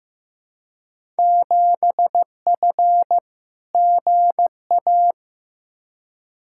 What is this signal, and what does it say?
Morse code, heard as a keyed tone, about 15 wpm, 700 Hz. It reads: 7F GA
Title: Field Day Exchanges Encore - 15wpm